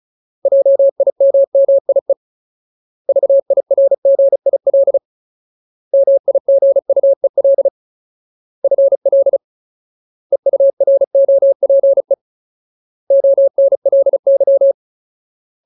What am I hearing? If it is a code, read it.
Morse code, heard as a keyed tone, about 34 wpm, 550 Hz. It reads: JIMMIE VIRGIL MIGUEL FL EUROPE ONLY
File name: Ultimate Word List / 35WPM Ultimate